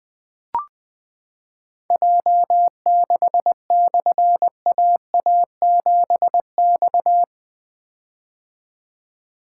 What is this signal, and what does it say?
Morse code, heard as a keyed tone, about 20 wpm, 700 Hz. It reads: J6/AA7X
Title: Difficult Call Signs 20wpm